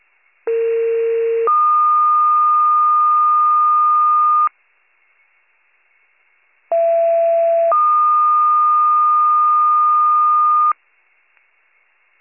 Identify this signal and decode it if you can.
Two-tone page, heard as a one-second tone, then a three-second tone, about 460 then 1220 Hz, about 670 then 1220 Hz. It is heard from 0.5 to 10.7 s.